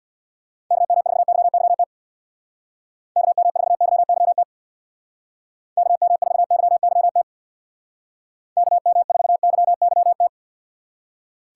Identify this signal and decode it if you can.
Morse code, heard as a keyed tone, about 62 wpm, 700 Hz. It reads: KM4YYT KM4YYT KM4YYT KM4YYT
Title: Speed Racing - USA 2x3 Call Signs 50wpm